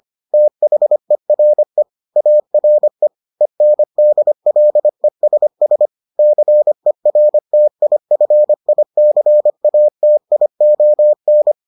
Morse code, heard as a keyed tone, about 25 wpm, 600 Hz. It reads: THERE ARE ENDLESS CERTIFICATION